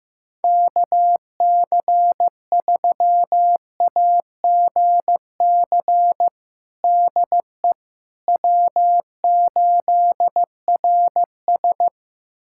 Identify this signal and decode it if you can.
Morse code, heard as a keyed tone, about 15 wpm, 700 Hz. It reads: KC3AGC DE W8RS
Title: USA Callsign DE Callsign 15wpm